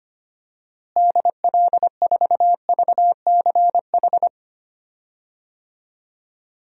Morse code, heard as a keyed tone, about 25 wpm, 700 Hz. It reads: DL4VCH
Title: DXCC-100 Call Signs - 6 Characters - Repeated 2x - 25wpm